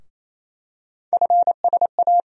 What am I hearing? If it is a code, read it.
Morse code, heard as a keyed tone, about 28 wpm, 700 Hz. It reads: FSA
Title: USA Call Sign Postfixes - 3 Letters 28wpm